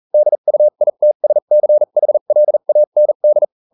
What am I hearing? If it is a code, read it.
Morse code, heard as a keyed tone, about 40 wpm, spaced as for 35 wpm, 600 Hz. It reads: DUITSCHLAND